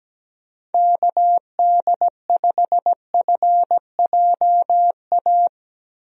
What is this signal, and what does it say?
Morse code, heard as a keyed tone, about 17 wpm, 700 Hz. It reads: KD5FJA
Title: USA Call Signs - 6 Characters - Repeated 2x - 17wpm